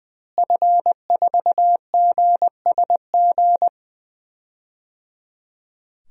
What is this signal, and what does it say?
Morse code, heard as a keyed tone, about 20 wpm, 700 Hz. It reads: F4GSG
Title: DXCC-100 Call Signs - 5 Characters - Repeated 3x - 20wpm